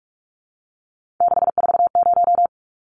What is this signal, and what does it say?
Morse code, heard as a keyed tone, about 44 wpm, 700 Hz. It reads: 640
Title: Numbers - 3-Digit 45wpm